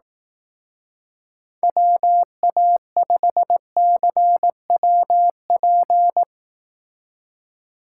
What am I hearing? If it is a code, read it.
Morse code, heard as a keyed tone, about 18 wpm, 700 Hz. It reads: WA5CWP
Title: Speed Racing - USA Call Signs 15wpm